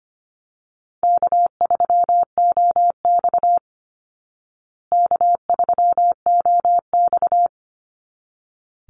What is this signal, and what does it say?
Morse code, heard as a keyed tone, about 25 wpm, 700 Hz. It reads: K3OX K3OX
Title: Easy Call Signs Repeated 2x - 25wpm